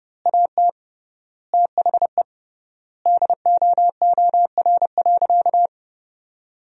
Morse code, heard as a keyed tone, about 30 wpm, 700 Hz. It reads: AT THE DOOR.
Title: 3x Word-Spacing - Sentences from Top 400 Words - 30wpm